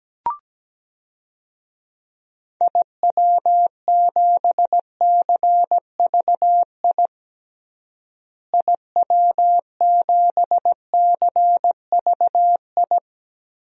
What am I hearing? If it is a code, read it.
Morse code, heard as a keyed tone, about 17 wpm, 700 Hz. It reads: IW7CVI IW7CVI
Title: DXCC-100 Call Signs - 6 Characters - Repeated 3x - 17wpm